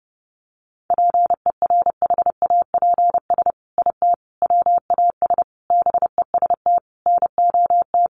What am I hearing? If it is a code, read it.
Morse code, heard as a keyed tone, about 30 wpm, 700 Hz. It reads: PERHAPS IT WAS BEST NOT
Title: Sentences from Top 600 Words 30wpm